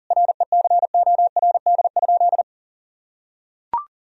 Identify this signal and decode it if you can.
Morse code, heard as a keyed tone, about 40 wpm, 700 Hz. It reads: RECORD?
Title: Sentences from Top 1000 Words - Review Part 3 - 40wpm